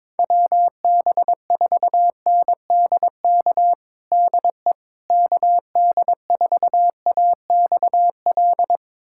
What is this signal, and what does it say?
Morse code, heard as a keyed tone, about 22 wpm, 700 Hz. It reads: WB4NDK DE KD4AXL
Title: USA Callsign DE Callsign - 6 Characters - 22wpm